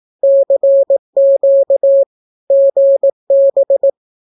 Morse code, heard as a keyed tone, about 18 wpm, 550 Hz. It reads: CQ GB
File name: naukapodstaw / skroty / skroty